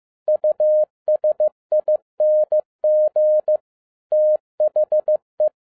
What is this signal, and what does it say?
Morse code, heard as a keyed tone, about 15 wpm, 610 Hz. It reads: USING THE
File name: 2011-10-mp3 / n07-A DOT 15wpm.mp3